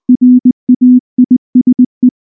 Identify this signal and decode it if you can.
Morse code, heard as a keyed tone, about 20 wpm, 250 Hz. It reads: RAISE
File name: morse / morse-training-20WPM-5